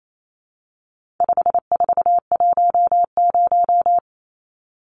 Morse code, heard as a keyed tone, about 28 wpm, 700 Hz. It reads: 5410
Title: Numbers - 4-Digit 28wpm